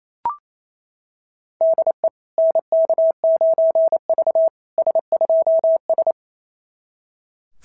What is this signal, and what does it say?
Morse code, heard as a keyed tone, about 24 wpm, 660 Hz. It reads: DE NK9V S2S
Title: SOTA QSO Practice 28wpm